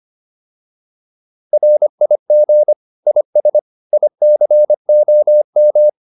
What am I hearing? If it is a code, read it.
Morse code, heard as a keyed tone, about 25 wpm, 600 Hz. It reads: RIG IS ICOM
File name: QSO307a 25